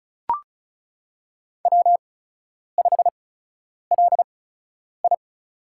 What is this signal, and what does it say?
Morse code, heard as a keyed tone, about 36 wpm, spaced as for 10 wpm, 700 Hz. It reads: W5LI
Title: USA Call Signs - 4 Characters - 8x Character Spacing - 35wpm